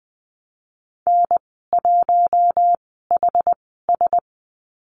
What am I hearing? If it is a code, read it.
Morse code, heard as a keyed tone, about 20 wpm, spaced as for 14 wpm, 700 Hz. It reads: N1HS
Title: 069 Lesson 12g - Letter d - Call Signs 20wpm